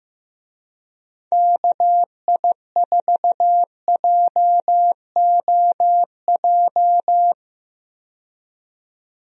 Morse code, heard as a keyed tone, about 16 wpm, 700 Hz. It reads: KI4JOJ
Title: USA Call Signs - 6 Characters - 15wpm